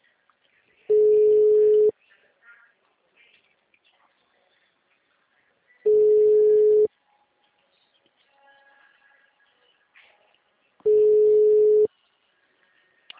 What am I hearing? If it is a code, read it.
Ringback tone (Europe, 425 Hz).